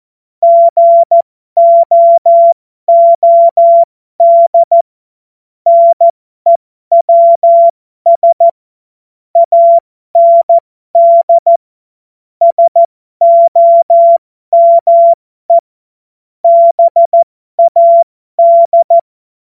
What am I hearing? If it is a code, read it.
Morse code, heard as a keyed tone, about 14 wpm, spaced as for 12 wpm, 680 Hz. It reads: GOOD NEWS AND SOME BAD
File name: Morsekurs / Witze